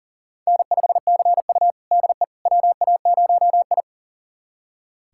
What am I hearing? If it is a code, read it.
Morse code, heard as a keyed tone, about 37 wpm, 700 Hz. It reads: N5CU DE WA0I
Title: USA Callsign DE Callsign - 4 Characters - 40wpm